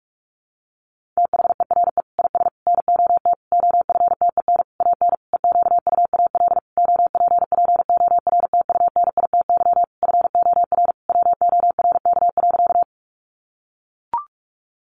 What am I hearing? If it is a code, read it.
Morse code, heard as a keyed tone, about 44 wpm, 700 Hz. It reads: THERE IS NOT OFTEN AN EQUAL OPPORTUNITY FOR WORK.
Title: Sentences from Top 1400 Words 45wpm